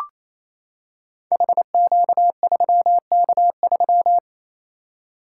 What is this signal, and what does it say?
Morse code, heard as a keyed tone, about 27 wpm, 700 Hz. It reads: HQ3K3
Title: Code Groups - 5 Letter:Numbers - 28wpm